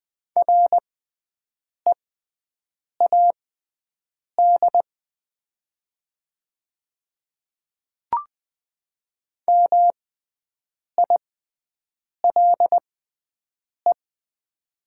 Morse code, heard as a keyed tone, about 20 wpm, spaced as for 7 wpm, 700 Hz. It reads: READ MILE
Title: 6x Character Spacing - Top 300 Words - 20wpm